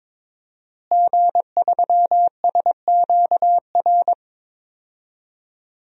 Morse code, heard as a keyed tone, about 22 wpm, 700 Hz. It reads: G3SQR
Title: DXCC-100 Call Signs - 5 Characters - Repeated 2x - 22wpm